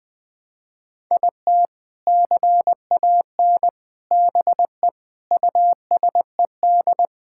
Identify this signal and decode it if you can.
Morse code, heard as a keyed tone, about 20 wpm, 700 Hz. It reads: IT CAN BE USED
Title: Sentences with 4-Word Phrases - 20wpm